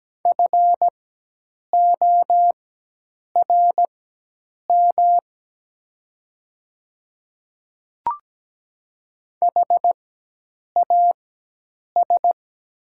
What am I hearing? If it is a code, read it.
Morse code, heard as a keyed tone, about 17 wpm, spaced as for 8 wpm, 700 Hz. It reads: FORM HAS
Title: 4x Character Spacing - Top 200 Words - 17wpm